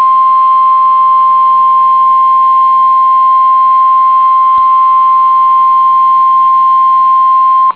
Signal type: Emergency Alert System (SAME) tones